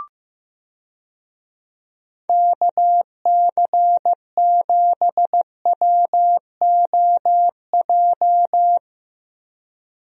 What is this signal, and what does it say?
Morse code, heard as a keyed tone, about 15 wpm, 700 Hz. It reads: KC7WOJ
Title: USA Call Signs - 6 Characters - Repeated 3x - 15wpm